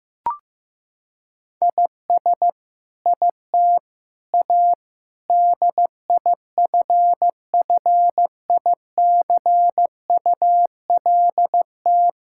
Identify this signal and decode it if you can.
Morse code, heard as a keyed tone, about 15 wpm, 700 Hz. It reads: IS IT A DIFFICULT
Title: Sentences from Top 1000 Words - Review Part 6 - 15wpm